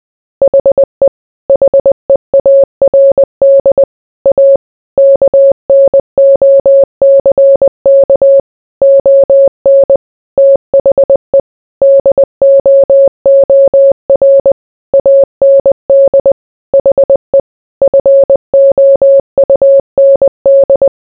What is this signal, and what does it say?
Morse code, heard as a keyed tone, about 20 wpm, 550 Hz. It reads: HE HEARD A KNOCK ON THE DOOR AND HE FOUND